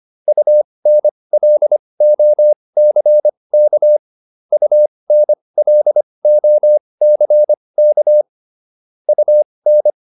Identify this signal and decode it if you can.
Morse code, heard as a keyed tone, about 25 wpm, spaced as for 20 wpm, 600 Hz. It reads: UNLOCK UNLOCK UN
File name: UN6 20